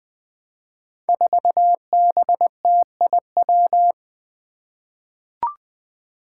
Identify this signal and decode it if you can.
Morse code, heard as a keyed tone, about 21 wpm, 700 Hz. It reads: 4BTIW
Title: Five Challenging Letter-Number Combinations 20wpm